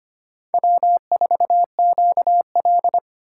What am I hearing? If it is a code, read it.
Morse code, heard as a keyed tone, about 25 wpm, 700 Hz. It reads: W4QL
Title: USA Call Signs - 4 Characters - 3x Character Spacing - 25wpm